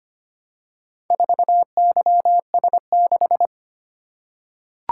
Morse code, heard as a keyed tone, about 25 wpm, 700 Hz. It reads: 4YS6
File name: practice / 25wpm / Four Letter-Number Combinations - 25wpm